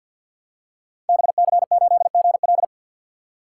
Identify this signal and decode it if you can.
Morse code, heard as a keyed tone, about 53 wpm, 700 Hz. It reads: BC8GL
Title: Code Groups - 5 Letter:Numbers - 50wpm